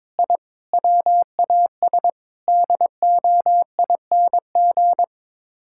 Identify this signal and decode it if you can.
Morse code, heard as a keyed tone, about 22 wpm, 700 Hz. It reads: I WAS DOING
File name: practice / 22wpm / Sets of 8 Words - 22wpm